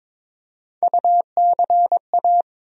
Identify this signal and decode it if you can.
Morse code, heard as a keyed tone, about 22 wpm, 700 Hz. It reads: UCA
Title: USA Call Sign Postfixes - 3 Letters 22wpm